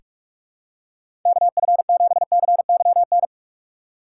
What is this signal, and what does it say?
Morse code, heard as a keyed tone, about 47 wpm, 700 Hz. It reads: KF7CYN
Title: USA Call Signs - 6 Characters - 6x Character Spacing - 45wpm